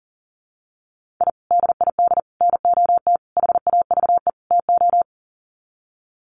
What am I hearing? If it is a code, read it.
Morse code, heard as a keyed tone, about 40 wpm, 700 Hz. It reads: I DID NOT HAVE TO
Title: Sets of 5 Words 40wpm